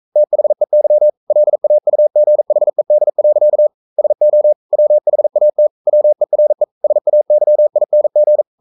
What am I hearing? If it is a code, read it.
Morse code, heard as a keyed tone, about 42 wpm, 600 Hz. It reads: THEY LAUGHED. SO WHAT WERE SAYING
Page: Romanen Never af Ken Follet på engelsk.